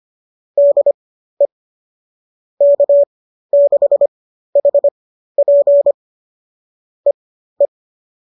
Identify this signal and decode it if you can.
Morse code, heard as a keyed tone, about 25 wpm, spaced as for 13 wpm, 570 Hz. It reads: DE K6HP EE